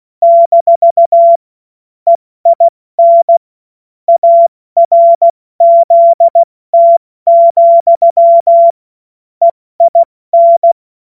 Morse code, heard as a keyed tone, about 16 wpm, 680 Hz. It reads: - EIN ARZT, EIN
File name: Morsekurs / Witze / Witz 21.November 2023 70 F80